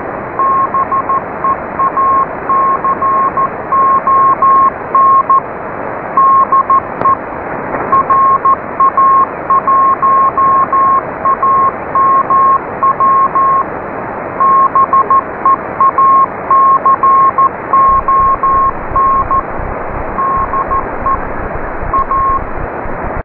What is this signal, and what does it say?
Morse code, heard as a keyed tone, about 14 wpm, 1080 Hz, with noise. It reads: BEACON DE RA1AMW BEACON DE A